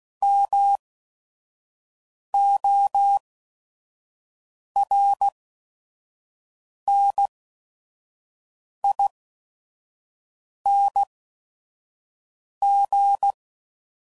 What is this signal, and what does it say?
Morse code, heard as a keyed tone, about 16 wpm, spaced as for 5 wpm, 800 Hz. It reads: MORNING
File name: CW Code Groups @5WPM / Weather Sets / Weather 3 17